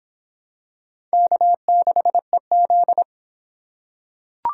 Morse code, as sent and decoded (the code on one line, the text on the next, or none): -.- -.... --..
K6EZ